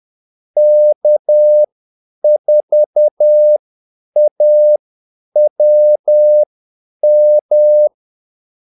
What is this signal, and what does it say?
Morse code, heard as a keyed tone, about 10 wpm, spaced as for 8 wpm, 600 Hz. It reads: K4AWM
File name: Callsigns-10-8wpm